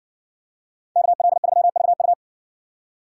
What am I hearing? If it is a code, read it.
Morse code, heard as a keyed tone, about 60 wpm, 700 Hz. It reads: KL3VU